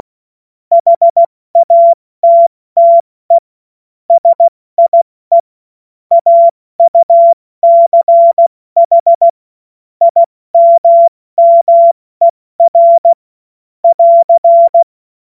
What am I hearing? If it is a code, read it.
Morse code, heard as a keyed tone, about 16 wpm, 680 Hz. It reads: HATTE SIE AUCH IMMER <AR>
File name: Morsekurs / Witze / Witz 09. Dezember 2025 70 F80